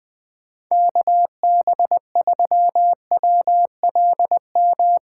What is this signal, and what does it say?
Morse code, heard as a keyed tone, about 19 wpm, 700 Hz. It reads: KB3WLM
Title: Speed Racing - USA 2x3 Call Signs 17wpm